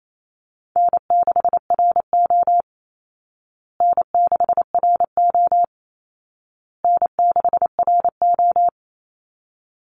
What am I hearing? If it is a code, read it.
Morse code, heard as a keyed tone, about 28 wpm, 700 Hz. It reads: N6RO N6RO N6RO
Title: Easy Call Signs Repeated 3x - 28wpm